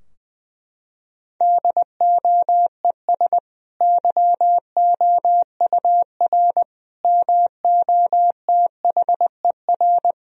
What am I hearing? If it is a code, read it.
Morse code, heard as a keyed tone, about 20 wpm, 700 Hz. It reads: DOES YOUR MOTHER
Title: Sentences from Top 1000 Words - Review Part 3 - 20wpm